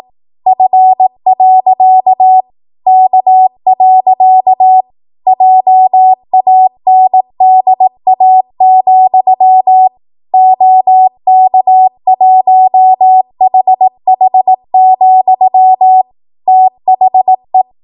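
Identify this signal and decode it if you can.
Morse code, heard as a keyed tone, about 18 wpm, 750 Hz. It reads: F. K. JANDA, OK1HH, THE
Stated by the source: Code practice at 18 WPM, a transition run bridging 15 to 20 WPM.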